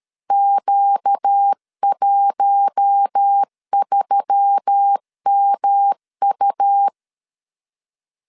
Morse code, heard as a keyed tone, about 13 wpm, 800 Hz. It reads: Q13MU
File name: LEZIONI / 6 PKX0 / 6 PK 12wpm 2006